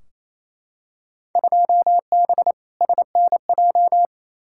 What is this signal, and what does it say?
Morse code, heard as a keyed tone, about 28 wpm, 700 Hz. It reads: 2B SNJ
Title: Field Day Exchanges Encore - 28wpm